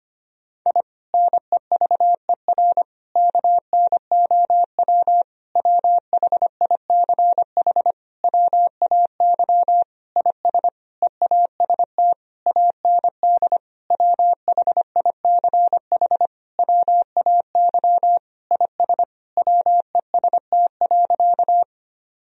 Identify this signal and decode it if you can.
Morse code, heard as a keyed tone, about 25 wpm, 700 Hz. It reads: I NEVER KNOW WHICH WAY IS EAST AND WHICH WAY IS WEST.
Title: Sentences from Top 1000 Words - Review Part 3 - 25wpm